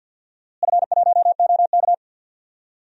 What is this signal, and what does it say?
Morse code, heard as a keyed tone, about 50 wpm, 700 Hz. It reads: F1OK
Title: DXCC-100 Call Signs - 4 Characters - Repeated 3x - 50wpm